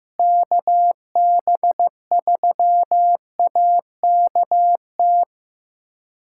Morse code, heard as a keyed tone, about 16 wpm, 700 Hz. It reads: KB3AKT
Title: USA Callsign DE Callsign 15wpm